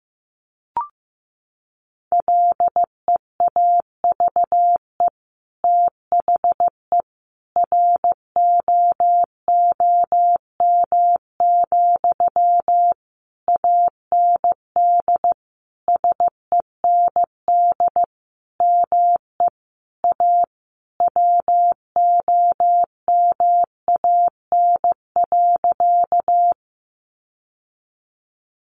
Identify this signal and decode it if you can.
Morse code, heard as a keyed tone, about 15 wpm, 700 Hz. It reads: LEAVE THE ROOM, AND SEND ME A WOMAN.
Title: Sentences from Top 1500 Words 15wpm